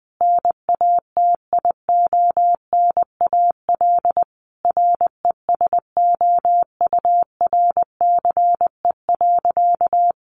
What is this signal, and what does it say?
Morse code, heard as a keyed tone, about 20 wpm, 700 Hz. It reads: NATIONAL RESOURCE.
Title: Sentences from Top 1300 Words 20wpm